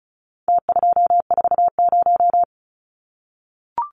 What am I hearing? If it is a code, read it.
Morse code, heard as a keyed tone, about 34 wpm, 700 Hz. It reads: T240
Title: Numbers - Contest Style - 4-Digit 35wpm